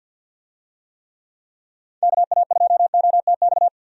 Morse code, heard as a keyed tone, about 51 wpm, 700 Hz. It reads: KA2OTX
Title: USA Call Signs - 6 Characters - 50wpm